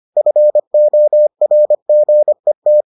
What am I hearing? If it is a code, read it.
Morse code, heard as a keyed tone, about 25 wpm, 600 Hz. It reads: FORGET